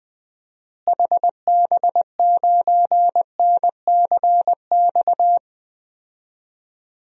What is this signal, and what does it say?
Morse code, heard as a keyed tone, about 21 wpm, 700 Hz. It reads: HB9NCX
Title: DXCC-100 Call Signs - 6 Characters - Repeated 2x - 20wpm